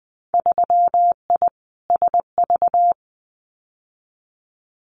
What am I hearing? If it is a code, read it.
Morse code, heard as a keyed tone, about 20 wpm, 700 Hz. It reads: 3I SV
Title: Winter Field Day Exchanges Encore 20wpm